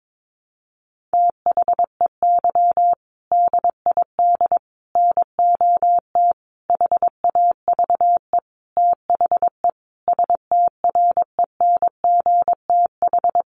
Morse code, heard as a keyed tone, about 22 wpm, 700 Hz. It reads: THEY DID NOT HAVE THE STRENGTH
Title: Sentences from Top 1400 Words 22wpm